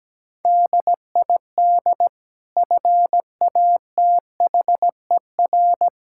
Morse code, heard as a keyed tone, about 17 wpm, 700 Hz. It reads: DID FATHER
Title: Sentences from Top 1600 Words 17wpm